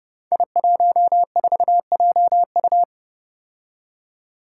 Morse code, heard as a keyed tone, about 30 wpm, 700 Hz. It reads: I14JU
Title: Five Challenging Letter-Number Combinations 30wpm